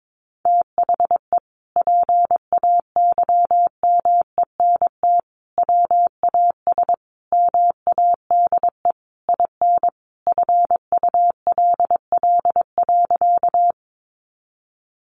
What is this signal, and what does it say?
Morse code, heard as a keyed tone, about 22 wpm, 700 Hz. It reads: THE PAYMENT WAS MADE IN FULL.
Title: Sentences from Top 1200 Words 22wpm